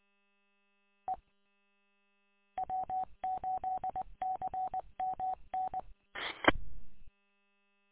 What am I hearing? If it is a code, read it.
Morse code, heard as a keyed tone, about 19 wpm, 750 Hz. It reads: E W8CMN E
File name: w8cmn- -combined